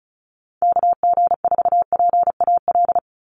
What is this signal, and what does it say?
Morse code, heard as a keyed tone, about 34 wpm, 700 Hz. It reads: KG4PAL
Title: USA Call Signs 35wpm